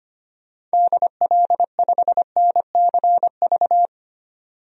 Morse code, heard as a keyed tone, about 25 wpm, 700 Hz. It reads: DL5NCV
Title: DXCC-100 Call Signs - 6 Characters - 25wpm